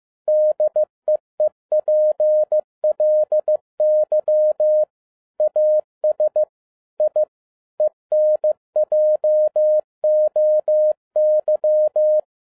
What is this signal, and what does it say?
Morse code, heard as a keyed tone, about 15 wpm, 610 Hz. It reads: DEEPLY AS I ENJOY
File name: FIELD DAY CONTEST QRV FROM SAKAE VILLAGE 15wpm